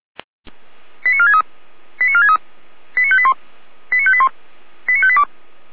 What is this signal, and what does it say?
Five-tone selective call (ZVEI), heard as quick runs of five tones, tones of about 70 ms. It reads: 89463 89463 89581 89581 89582